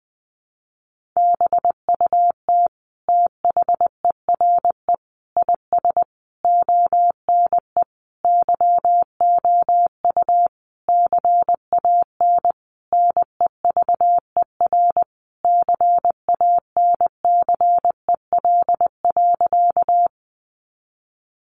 Morse code, heard as a keyed tone, about 20 wpm, 700 Hz. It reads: BUT THERE IS ONE YOU CAN NEVER CANCEL.
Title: Sentences from Top 1200 Words 20wpm